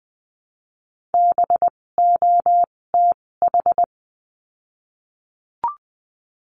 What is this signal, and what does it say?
Morse code, heard as a keyed tone, about 20 wpm, spaced as for 16 wpm, 700 Hz. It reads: BOTH
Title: 256 Lesson 42a - Words 12-20wpm